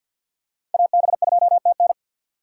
Morse code, heard as a keyed tone, about 51 wpm, 700 Hz. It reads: AB2TN
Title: USA Call Signs - 6x Character Spacing - 50wpm